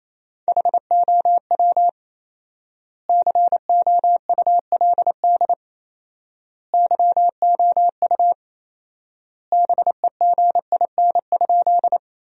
Morse code, heard as a keyed tone, about 28 wpm, 700 Hz. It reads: HOW COULD YOU BEGIN?
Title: 4x Word-Spacing - Sentences from Top 300 Words - 28wpm